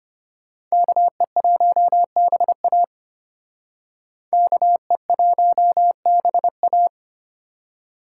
Morse code, heard as a keyed tone, about 32 wpm, 700 Hz. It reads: KE1BA KE1BA